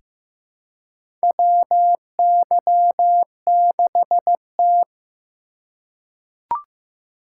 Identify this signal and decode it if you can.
Morse code, heard as a keyed tone, about 15 wpm, 700 Hz. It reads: WY6T
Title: USA Call Signs - 4 Characters - 2x Character Spacing - 15wpm